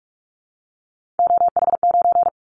Morse code, heard as a keyed tone, about 46 wpm, 700 Hz. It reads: OH9
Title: Callsign Prefixes for Top 50 DXCC - 45wpm